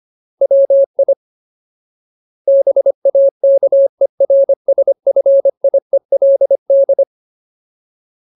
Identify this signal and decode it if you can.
Morse code, heard as a keyed tone, about 25 wpm, 550 Hz. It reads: WI BAKERSFIELD
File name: Ultimate Word List / 25WPM Ultimate